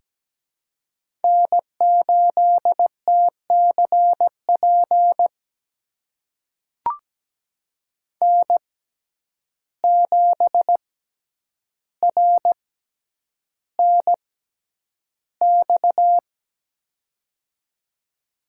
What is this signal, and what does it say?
Morse code, heard as a keyed tone, about 17 wpm, spaced as for 9 wpm, 700 Hz. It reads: N8TCP N7RNX
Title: USA Call Signs - 5 Characters - 6x Character Spacing - 17wpm